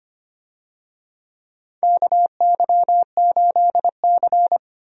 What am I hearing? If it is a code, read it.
Morse code, heard as a keyed tone, about 25 wpm, 700 Hz. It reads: KY8C
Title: USA Call Signs - 4 Characters - Repeated 3x - 25wpm